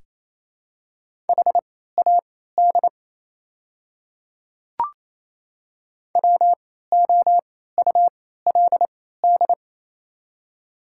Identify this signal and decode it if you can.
Morse code, heard as a keyed tone, about 28 wpm, spaced as for 16 wpm, 700 Hz. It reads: HAD WOULD
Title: 3x Character Spacing - Top 300 Words - 28wpm